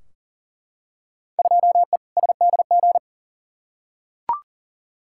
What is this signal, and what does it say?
Morse code, heard as a keyed tone, about 40 wpm, 700 Hz. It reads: 2E SDG
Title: Field Day Exchanges - 40wpm